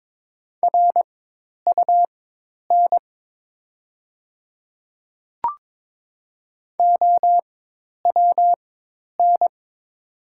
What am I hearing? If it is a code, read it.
Morse code, heard as a keyed tone, about 22 wpm, spaced as for 10 wpm, 690 Hz. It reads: RUN OWN
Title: 4x Character Spacing - Top 300 Words - 22wpm